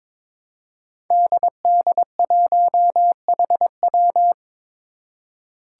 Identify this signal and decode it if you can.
Morse code, heard as a keyed tone, about 22 wpm, 700 Hz. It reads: DD1HW
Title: Five Challenging Letter-Number Combinations 22wpm